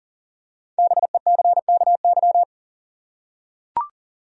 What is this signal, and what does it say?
Morse code, heard as a keyed tone, about 39 wpm, 700 Hz. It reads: BECKY
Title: USA Common Names - 40wpm